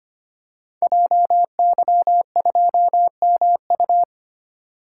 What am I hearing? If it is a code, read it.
Morse code, heard as a keyed tone, about 25 wpm, 700 Hz. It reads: JY2MU
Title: Code Groups - 5 Letter:Numbers - 25wpm